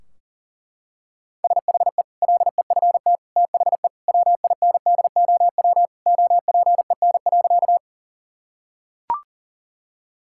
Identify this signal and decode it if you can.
Morse code, heard as a keyed tone, about 40 wpm, 700 Hz. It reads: SHE LEFT THE WINDOW OPEN.
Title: Sentences from Top 1600 Words 40wpm